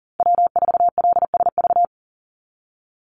Morse code, heard as a keyed tone, about 40 wpm, 700 Hz. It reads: W4LSV
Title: Call Signs Repeated 2x - 40wpm